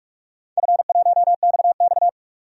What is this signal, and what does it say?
Morse code, heard as a keyed tone, about 46 wpm, 700 Hz. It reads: F1XX
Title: DXCC-100 Call Signs - 4 Characters - 45wpm